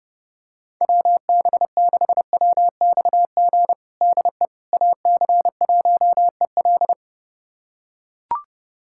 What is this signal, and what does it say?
Morse code, heard as a keyed tone, about 30 wpm, 700 Hz. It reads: WB6WXG DE AC1EL
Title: USA Callsign DE Callsign 30wpm